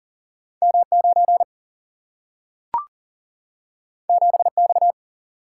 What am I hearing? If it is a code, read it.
Morse code, heard as a keyed tone, about 40 wpm, 700 Hz. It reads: M 9 7 X
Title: Single-Letter Single-Number 40wpm